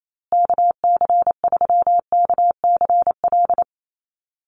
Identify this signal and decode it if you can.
Morse code, heard as a keyed tone, about 28 wpm, 700 Hz. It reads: KC3KCL
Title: USA Call Signs Repeated 3x 28wpm